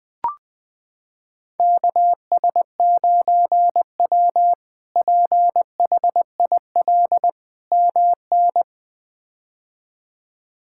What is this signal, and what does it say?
Morse code, heard as a keyed tone, about 20 wpm, 700 Hz. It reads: KS9W PHIL MN
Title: K1USN SST QSO Practice - 20wpm